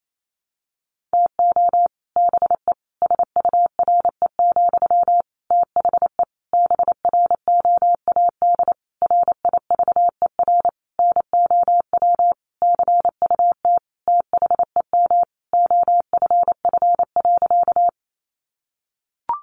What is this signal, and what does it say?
Morse code, heard as a keyed tone, about 28 wpm, 700 Hz. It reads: TO BE SURE, THE BROAD RIVER NOW CUT THEM OFF.
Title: Sentences from Top 1000 Words 28wpm